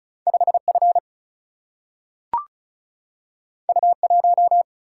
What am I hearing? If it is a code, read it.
Morse code, heard as a keyed tone, about 36 wpm, 700 Hz. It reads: 5 F U 1
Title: Single-Letter Single-Number 35wpm